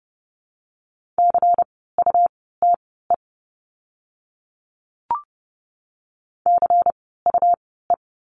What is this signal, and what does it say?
Morse code, heard as a keyed tone, about 30 wpm, spaced as for 17 wpm, 700 Hz. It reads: CUTE CUE
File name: Lesson 17f - Letter u - Words 30wpm